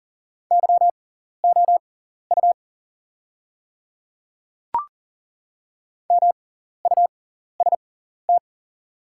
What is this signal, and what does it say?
Morse code, heard as a keyed tone, about 40 wpm, spaced as for 14 wpm, 700 Hz. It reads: YOU MUST
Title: 6x Character Spacing - Top 200 Words - 40wpm